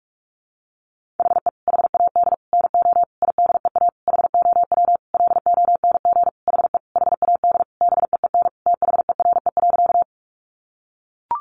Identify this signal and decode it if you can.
Morse code, heard as a keyed tone, about 44 wpm, 700 Hz. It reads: HE HAD NO IDEA HOW LONG HE HAD BEEN THERE.
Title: Sentences from Top 300 Words 45wpm